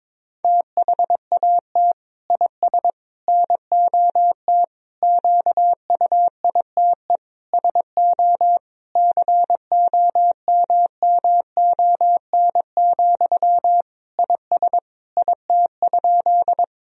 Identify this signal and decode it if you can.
Morse code, heard as a keyed tone, about 22 wpm, 700 Hz. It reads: THAT IS NOT QUITE SO COMMON, IS IT?
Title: Sentences from Top 1000 Words - Review Part 3 - 22wpm